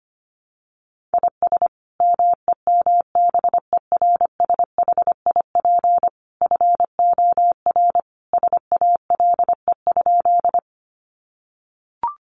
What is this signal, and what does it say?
Morse code, heard as a keyed tone, about 25 wpm, 700 Hz. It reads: IS MEMBERSHIP FOR SALE?
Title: Sentences from Top 1400 Words 25wpm